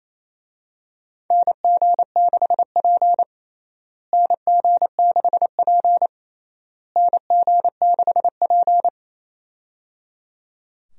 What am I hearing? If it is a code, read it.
Morse code, heard as a keyed tone, about 28 wpm, 700 Hz. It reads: NG6P NG6P NG6P
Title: USA Call Signs - 4 Characters - Repeated 3x - 28wpm